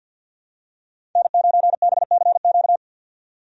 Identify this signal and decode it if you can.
Morse code, heard as a keyed tone, about 50 wpm, 700 Hz. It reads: N9BCQ
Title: USA Call Signs - 5 Characters - 50wpm